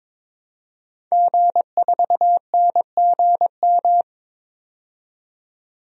Morse code, heard as a keyed tone, about 22 wpm, 700 Hz. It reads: G4NGM
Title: DXCC-100 Call Signs - 5 Characters - 22wpm